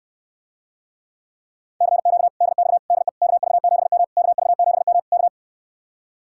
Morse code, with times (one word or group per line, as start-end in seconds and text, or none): 1.8-2.3 CQ
2.4-2.8 DX
2.9-3.1 DE
3.2-4.0 KV7M
4.2-5.0 KV7M
5.1-5.3 K